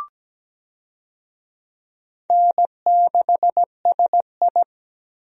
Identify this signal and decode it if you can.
Morse code, heard as a keyed tone, about 17 wpm, 700 Hz. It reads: N6SI
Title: USA Call Signs - 4 Characters - Repeated 3x - 17wpm